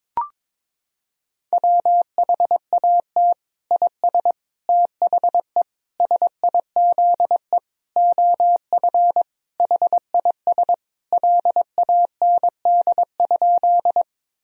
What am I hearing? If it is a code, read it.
Morse code, heard as a keyed tone, about 21 wpm, 700 Hz. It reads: WHAT IS THE SIZE OF HIS LAND?
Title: Sentences from Top 1000 Words - Review Part 2 - 22wpm